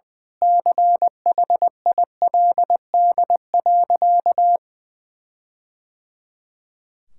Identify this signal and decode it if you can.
Morse code, heard as a keyed tone, about 20 wpm, 700 Hz. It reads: CHILD.
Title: Sentences from Top 1600 Words 20wpm